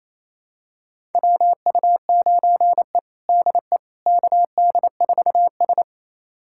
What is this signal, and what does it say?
Morse code, heard as a keyed tone, about 28 wpm, 700 Hz. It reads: WU9E DE KD4S
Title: USA Callsign DE Callsign - 4 Characters - 28wpm